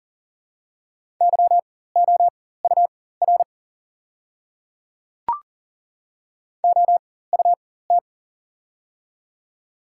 Morse code, heard as a keyed tone, about 40 wpm, spaced as for 19 wpm, 700 Hz. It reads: YOUR OUT
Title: 4x Character Spacing - Top 100 Words - 40wpm